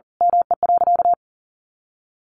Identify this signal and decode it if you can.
Morse code, heard as a keyed tone, about 40 wpm, 700 Hz. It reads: ME.
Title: Sentences from Top 500 Words 40wpm